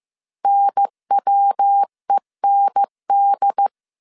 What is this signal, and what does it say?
Morse code, heard as a keyed tone, about 15 wpm, 800 Hz. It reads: NWEND